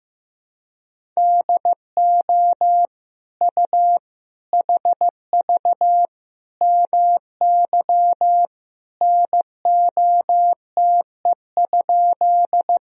Morse code, heard as a keyed tone, about 15 wpm, 700 Hz. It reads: DO U HV MY NOTE?
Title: Sentences from Top 500 Words and Abbreviations 15wpm